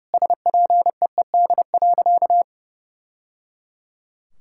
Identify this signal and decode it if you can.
Morse code, heard as a keyed tone, about 30 wpm, 700 Hz. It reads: SPEED.
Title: Sentences from Top 1000 Words - Review Part 5 - 30wpm